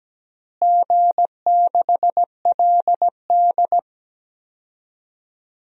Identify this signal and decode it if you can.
Morse code, heard as a keyed tone, about 17 wpm, 700 Hz. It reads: G6LD
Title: DXCC-100 Call Signs - 4 Characters - Repeated 3x - 17wpm